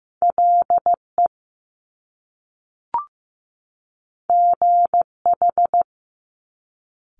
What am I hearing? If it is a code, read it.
Morse code, heard as a keyed tone, about 15 wpm, 700 Hz. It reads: LE GH
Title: Top Two Letter Combinations in Words 15wpm